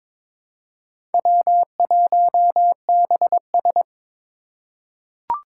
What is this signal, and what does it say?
Morse code, heard as a keyed tone, about 22 wpm, 700 Hz. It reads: W1BS
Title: Four Challenging Letter-Number Combinations 22wpm